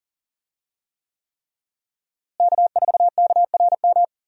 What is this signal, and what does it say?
Morse code, heard as a keyed tone, about 40 wpm, 700 Hz. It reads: K4KRM
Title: USA Call Signs - 5 Characters - Repeated 3x - 40wpm